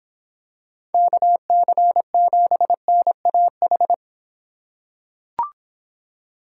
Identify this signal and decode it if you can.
Morse code, heard as a keyed tone, about 24 wpm, 700 Hz. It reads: KC7NAH